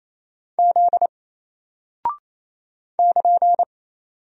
Morse code, heard as a keyed tone, about 27 wpm, 700 Hz. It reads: Z <KN>
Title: Single Character (Letter:Number:Punctuation:Prosign) 28wpm